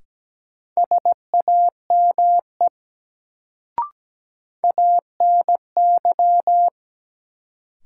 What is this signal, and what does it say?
Morse code, heard as a keyed tone, about 17 wpm, 700 Hz. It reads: SAME ANY
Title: Sentences from Top 200 Words - New Word Drill - 17wpm